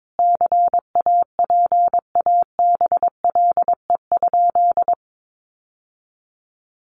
Morse code, heard as a keyed tone, about 22 wpm, 700 Hz. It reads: CAPABLE?
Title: Sentences from Top 1200 Words 22wpm